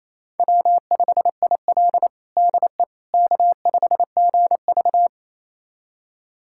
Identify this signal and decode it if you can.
Morse code, heard as a keyed tone, about 28 wpm, 700 Hz. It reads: W5IL DE K5GV
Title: USA Callsign DE Callsign - 4 Characters - 28wpm